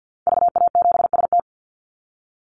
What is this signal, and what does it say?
Morse code, heard as a keyed tone, about 50 wpm, 700 Hz. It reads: VA7ST
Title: Call Signs Repeated 3x - 50wpm